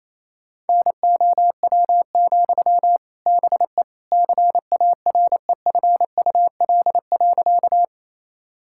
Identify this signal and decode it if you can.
Morse code, heard as a keyed tone, about 28 wpm, 700 Hz. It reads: NOW, BE CAREFUL.
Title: Sentences from Top 1600 Words 28wpm